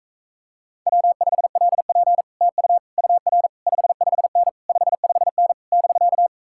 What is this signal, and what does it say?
Morse code, heard as a keyed tone, about 44 wpm, 700 Hz. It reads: W5LP TU UR 55N 55N <BK>